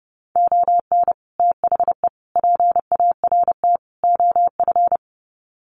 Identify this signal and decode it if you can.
Morse code, heard as a keyed tone, about 30 wpm, 700 Hz. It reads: ON THE PART OF
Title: Sets of 4 Words 30wpm